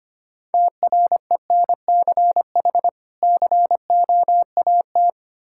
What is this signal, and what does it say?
Morse code, heard as a keyed tone, about 25 wpm, 700 Hz. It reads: TRENCH COAT
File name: practice / 25wpm / Difficult Words with Hints - 25wpm